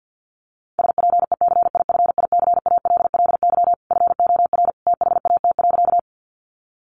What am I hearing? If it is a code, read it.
Morse code, heard as a keyed tone, about 50 wpm, 700 Hz. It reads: SPECIFICALLY FOR THAT.
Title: Sentences from Top 1100 Words 50wpm